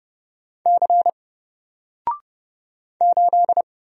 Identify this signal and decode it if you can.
Morse code, heard as a keyed tone, about 30 wpm, 700 Hz. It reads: C 8
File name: Single-Letter-Number-30wpm